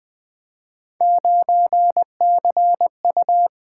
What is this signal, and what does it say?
Morse code, heard as a keyed tone, about 20 wpm, 700 Hz. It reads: 9CU
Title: Three Letter-Number Combinations - 20wpm